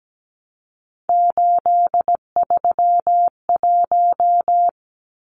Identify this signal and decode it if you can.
Morse code, heard as a keyed tone, about 17 wpm, 700 Hz. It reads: 831
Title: Numbers - 3-Digit 17wpm